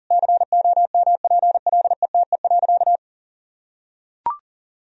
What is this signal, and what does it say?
Morse code, heard as a keyed tone, about 40 wpm, 700 Hz. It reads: COMPLETE.
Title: Sentences from Top 1000 Words - Review Part 2 - 40wpm